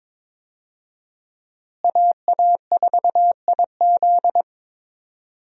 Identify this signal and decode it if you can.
Morse code, heard as a keyed tone, about 22 wpm, 700 Hz. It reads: AA4IZ
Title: USA Call Signs - 5 Characters - Repeated 3x - 22wpm